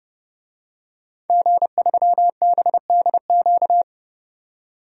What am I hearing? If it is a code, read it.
Morse code, heard as a keyed tone, about 30 wpm, 700 Hz. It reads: G3BDQ
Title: DXCC-100 Call Signs - 5 Characters - 30wpm